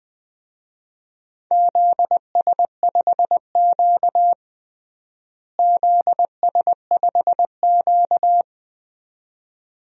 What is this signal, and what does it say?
Morse code, heard as a keyed tone, about 20 wpm, 700 Hz. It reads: ZS5Q ZS5Q
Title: DXCC-100 Call Signs - 4 Characters - Repeated 2x - 20wpm